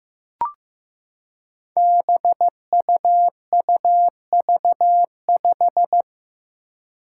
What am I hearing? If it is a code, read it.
Morse code, heard as a keyed tone, about 16 wpm, 700 Hz. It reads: BUUV5
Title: Five Challenging Letter-Number Combinations 15wpm